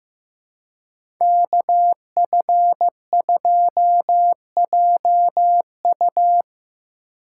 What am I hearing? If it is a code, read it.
Morse code, heard as a keyed tone, about 15 wpm, 700 Hz. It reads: KF2JU
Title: USA Call Signs - 5 Characters - 2x Character Spacing - 15wpm